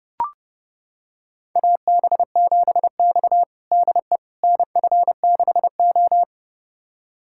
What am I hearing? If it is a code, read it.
Morse code, heard as a keyed tone, about 30 wpm, 700 Hz. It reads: AB7X DE NF6O
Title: USA Callsign DE Callsign - 4 Characters - 30wpm